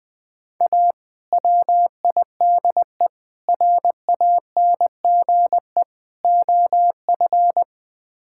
Morse code, heard as a keyed tone, about 20 wpm, 690 Hz. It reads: A WIDE RANGE OF
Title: Build A Phrase - 4-Word Phrases - 20wpm